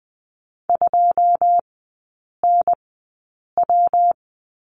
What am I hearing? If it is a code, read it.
Morse code, heard as a keyed tone, about 20 wpm, spaced as for 8 wpm, 700 Hz. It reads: 2NW
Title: 138 Lesson 23e - Letter f - Three Characters 20wpm